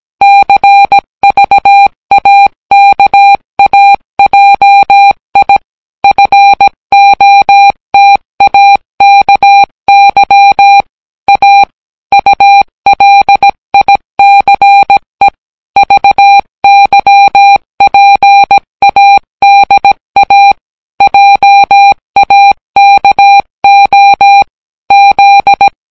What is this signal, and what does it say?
Morse code, heard as a keyed tone, about 17 wpm, 800 Hz. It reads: CVAKAJI FOTAKY A ULICE VYPADA JAKO Z